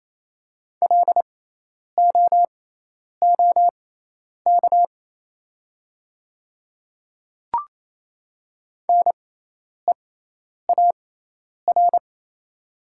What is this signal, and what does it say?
Morse code, heard as a keyed tone, about 28 wpm, spaced as for 10 wpm, 690 Hz. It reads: LOOK NEAR
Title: 6x Character Spacing - Top 200 Words - 28wpm